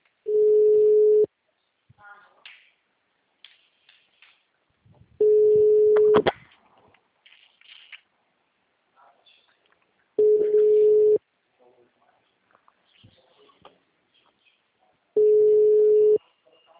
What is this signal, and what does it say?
Ringback tone (Europe, 425 Hz), with speech.